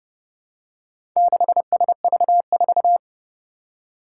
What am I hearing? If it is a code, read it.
Morse code, heard as a keyed tone, about 29 wpm, 700 Hz. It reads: 6SV4
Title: Four Challenging Letter-Number Combinations 30wpm